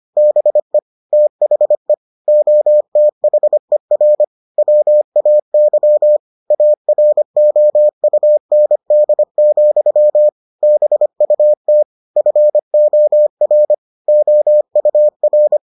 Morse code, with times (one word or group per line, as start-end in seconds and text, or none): 0.2-0.8 BE
1.1-1.9 THE
2.3-4.2 OTHER
4.6-6.2 WAY
6.5-10.3 AROUND,
10.6-11.8 BUT
12.2-13.7 FOR
14.1-15.6 OUR